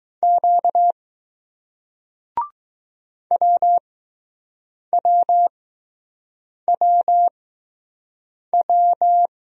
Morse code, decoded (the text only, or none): Q WWWW